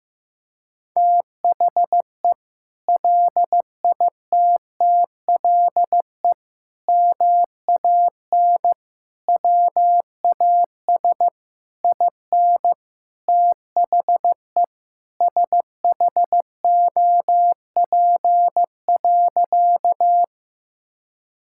Morse code, heard as a keyed tone, about 15 wpm, 700 Hz. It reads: THE LITTLE MAN WAS IN THE SHOP.
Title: Sentences from Top 1600 Words 15wpm